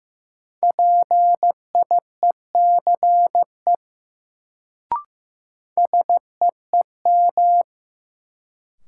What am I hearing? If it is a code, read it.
Morse code, heard as a keyed tone, about 15 wpm, 700 Hz. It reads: PIECE SEEM
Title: Sets of 3 Words - New Word Drill - 15wpm